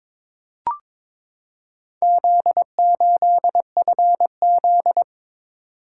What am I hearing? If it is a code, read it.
Morse code, heard as a keyed tone, about 22 wpm, 700 Hz. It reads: Z8FZ
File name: practice / 22wpm / Four Letter-Number Combinations - 22wpm